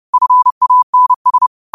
Morse code, heard as a keyed tone, about 30 wpm, 1000 Hz. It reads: FANS